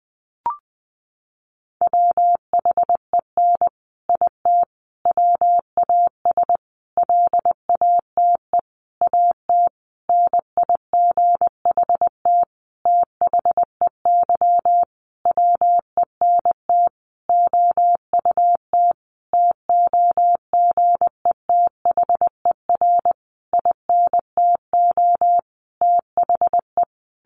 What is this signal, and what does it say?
Morse code, heard as a keyed tone, about 20 wpm, 700 Hz. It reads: WHEN IT WAS LATE AT NIGHT THEY WENT OUT TOGETHER INTO THE
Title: Sentences from Top 600 Words 20wpm